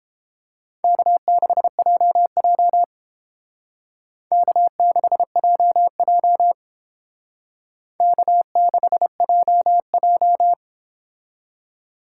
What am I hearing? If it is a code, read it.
Morse code, heard as a keyed tone, about 30 wpm, 700 Hz. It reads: K6JJ K6JJ K6JJ